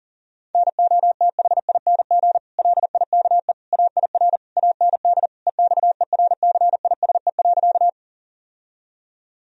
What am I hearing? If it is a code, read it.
Morse code, heard as a keyed tone, about 37 wpm, 700 Hz. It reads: NOTHING LIKE AIR AND EXERCISE.
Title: Sentences from Top 1000 Words - Review Part 4 - 40wpm